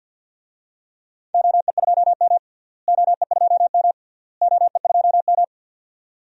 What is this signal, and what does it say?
Morse code, heard as a keyed tone, about 50 wpm, 700 Hz. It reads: OE2M OE2M OE2M
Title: DXCC-100 Call Signs - 4 Characters - Repeated 3x - 50wpm